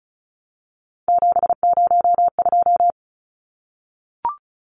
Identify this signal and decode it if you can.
Morse code, heard as a keyed tone, about 34 wpm, 700 Hz. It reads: 702
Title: Numbers - 3-Digit 35wpm